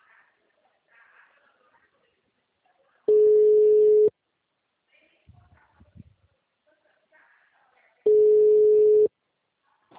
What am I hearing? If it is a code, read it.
Ringback tone (Europe, 425 Hz).
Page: Dracena (Call-Center)